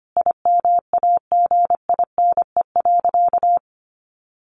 Morse code, heard as a keyed tone, about 25 wpm, 700 Hz. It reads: IMAGINE.
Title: Sentences from Top 1400 Words 25wpm